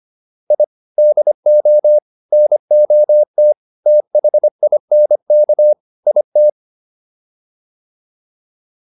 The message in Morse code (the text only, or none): I DO NOT THINK IT